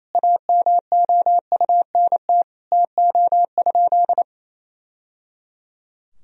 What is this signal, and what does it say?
Morse code, heard as a keyed tone, about 28 wpm, 700 Hz. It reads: AMOUNT TO?
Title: Sentences from Top 1000 Words - Review Part 4 - 28wpm